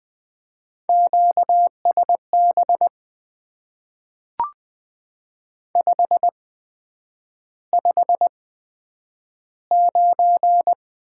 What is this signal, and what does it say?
Morse code, heard as a keyed tone, about 20 wpm, spaced as for 9 wpm, 700 Hz. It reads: QSB 559
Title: Top 90 QSO Elements - 8x Character Spacing - 20wpm